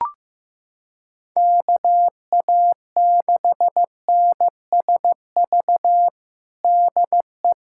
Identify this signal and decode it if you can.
Morse code, heard as a keyed tone, about 15 wpm, 700 Hz. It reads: KA6NSV DE
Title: USA Callsign DE Callsign 15wpm